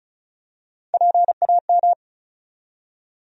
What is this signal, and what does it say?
Morse code, heard as a keyed tone, about 36 wpm, 700 Hz. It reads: PAM
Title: USA Common Names - 35wpm